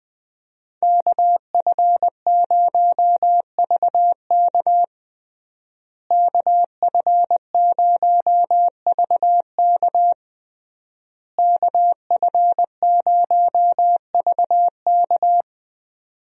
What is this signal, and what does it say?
Morse code, heard as a keyed tone, about 20 wpm, 700 Hz. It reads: KF0VK KF0VK KF0VK
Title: USA Call Signs - 5 Characters - Repeated 3x - 20wpm